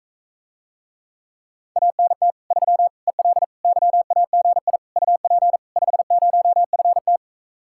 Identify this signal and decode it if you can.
Morse code, heard as a keyed tone, about 45 wpm, 700 Hz. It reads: ANT 3 EL YAGI UP 50FT